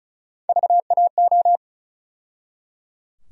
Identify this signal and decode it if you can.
Morse code, heard as a keyed tone, about 34 wpm, 700 Hz. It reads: VAO
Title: USA Call Sign Postfixes - 3 Letters 35wpm